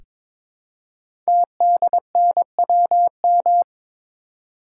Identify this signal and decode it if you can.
Morse code, heard as a keyed tone, about 22 wpm, 700 Hz. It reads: TDNWM
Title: Code Groups - 5 Letters - 22wpm